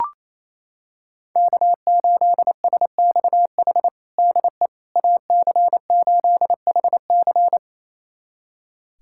Morse code, heard as a keyed tone, about 28 wpm, 700 Hz. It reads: K8SXH DE AC8HC
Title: USA Callsign DE Callsign 28wpm